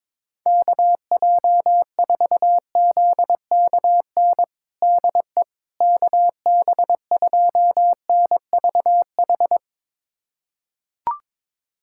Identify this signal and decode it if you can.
Morse code, heard as a keyed tone, about 22 wpm, 700 Hz. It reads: KJ4ZKN DE KB2NVH
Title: USA Callsign DE Callsign - 6 Characters - 22wpm